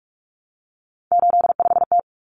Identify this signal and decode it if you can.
Morse code, heard as a keyed tone, about 44 wpm, 700 Hz. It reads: 85T
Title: Common Cut Numbers - 3-Digit 45wpm